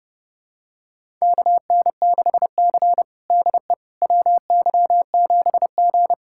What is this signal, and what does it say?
Morse code, heard as a keyed tone, about 30 wpm, 700 Hz. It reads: KN6C DE WY7G
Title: USA Callsign DE Callsign - 4 Characters - 30wpm